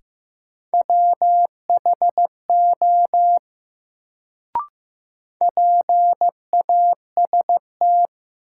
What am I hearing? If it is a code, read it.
Morse code, heard as a keyed tone, about 15 wpm, 700 Hz. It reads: WHO PAST
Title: Sets of 2 Words - New Word Drill - 15wpm